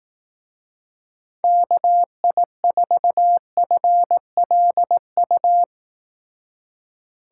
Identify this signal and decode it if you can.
Morse code, heard as a keyed tone, about 18 wpm, 700 Hz. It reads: KI4FLU
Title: Speed Racing - USA Call Signs 15wpm